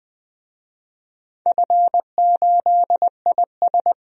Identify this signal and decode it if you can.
Morse code, heard as a keyed tone, about 20 wpm, 700 Hz. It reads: F8IS
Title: DXCC-100 Call Signs - 4 Characters - 20wpm